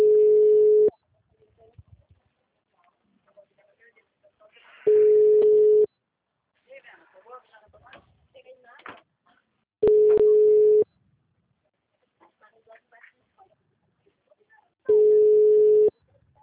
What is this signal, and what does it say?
Ringback tone (Europe, 425 Hz), with speech.
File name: out- -2005- - - .56375